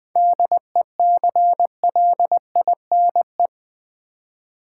Morse code, heard as a keyed tone, about 20 wpm, 690 Hz. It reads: DECLINE
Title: 7 Letter Words - First 3 Letters Given - 20wpm